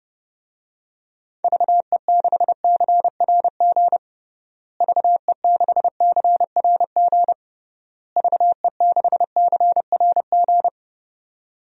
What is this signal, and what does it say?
Morse code, heard as a keyed tone, about 30 wpm, 700 Hz. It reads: VE6CRG VE6CRG VE6CRG
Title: DXCC-100 Call Signs - 6 Characters - Repeated 3x - 30wpm